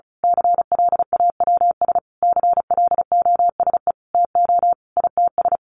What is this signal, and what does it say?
Morse code, heard as a keyed tone, about 34 wpm, 700 Hz. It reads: CLAWS CLOSE TO ITS
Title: A Sound Of Thunder 35wpm